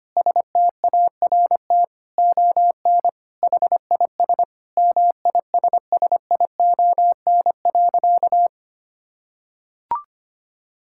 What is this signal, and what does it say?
Morse code, heard as a keyed tone, about 25 wpm, 700 Hz. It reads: START ON HIS MISSION.
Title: Sentences from Top 1600 Words 25wpm